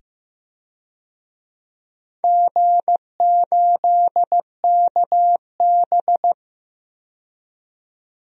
Morse code, heard as a keyed tone, about 15 wpm, 700 Hz. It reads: G8KB
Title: DXCC-100 Call Signs - 4 Characters - 15wpm